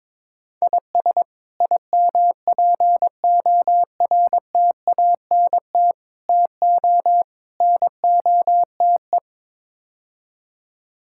Morse code, heard as a keyed tone, about 22 wpm, 700 Hz. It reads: IS IMPORTANT TO NOTE
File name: practice / 22wpm / Build A Phrase - 5-Word Phrases - 22wpm